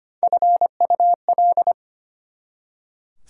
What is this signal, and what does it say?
Morse code, heard as a keyed tone, about 25 wpm, 700 Hz. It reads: FUL
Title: Prefixes and Suffixes 25wpm